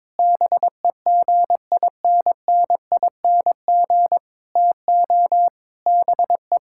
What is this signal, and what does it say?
Morse code, heard as a keyed tone, about 22 wpm, 700 Hz. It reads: BEGINNING TO BE
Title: Sentences from Top 1000 Words - Review Part 5 - 22wpm